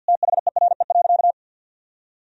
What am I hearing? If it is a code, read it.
Morse code, heard as a keyed tone, about 50 wpm, 700 Hz. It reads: THERE.
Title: Sentences from Top 1000 Words - Review Part 2 - 50wpm